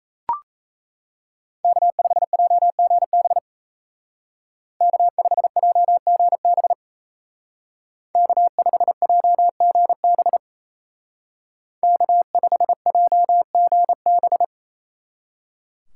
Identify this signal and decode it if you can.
Morse code, heard as a keyed tone, about 33 wpm, 700 Hz. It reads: K5JGB K5JGB K5JGB K5JGB
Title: Speed Racing - USA 1x3 Call Signs 28wpm